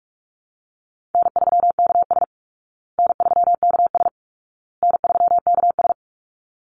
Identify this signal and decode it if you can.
Morse code, heard as a keyed tone, about 44 wpm, 700 Hz. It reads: N3KS N3KS N3KS
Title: Easy Call Signs Repeated 3x - 45wpm